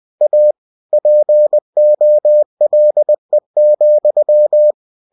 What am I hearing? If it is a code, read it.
Morse code, heard as a keyed tone, about 20 wpm, 600 Hz. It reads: A POLE,
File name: cw-books / winnie-the-pooh / winnie-20wpm-page-0085